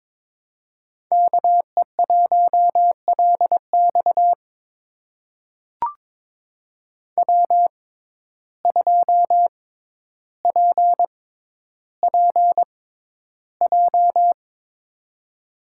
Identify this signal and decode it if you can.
Morse code, heard as a keyed tone, about 22 wpm, spaced as for 11 wpm, 700 Hz. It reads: KE1LX W2PPJ
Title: USA Call Signs - 5 Characters - 6x Character Spacing - 22wpm